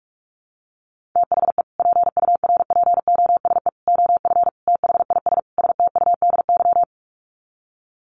Morse code, heard as a keyed tone, about 44 wpm, 700 Hz. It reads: THE PURPOSE OF THIS STUDY
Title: Sets of 5 Words 45wpm